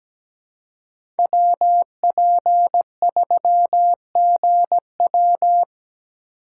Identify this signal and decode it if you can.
Morse code, heard as a keyed tone, about 17 wpm, 700 Hz. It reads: WP3GW
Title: USA Call Signs - 5 Characters - 6x Character Spacing - 17wpm